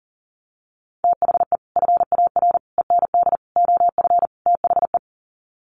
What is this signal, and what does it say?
Morse code, heard as a keyed tone, about 40 wpm, 700 Hz. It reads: THE FAR END OF THE
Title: Sets of 5 Words 40wpm